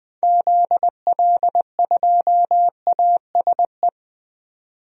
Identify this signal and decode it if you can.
Morse code, heard as a keyed tone, about 20 wpm, 700 Hz. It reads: ZL2ASE
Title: DXCC-100 Call Signs - 6 Characters - 20wpm